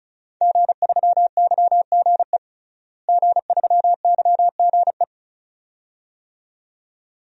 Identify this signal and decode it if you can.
Morse code, heard as a keyed tone, about 34 wpm, 700 Hz. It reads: G3YGE G3YGE
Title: DXCC-100 Call Signs - 5 Characters - Repeated 2x - 35wpm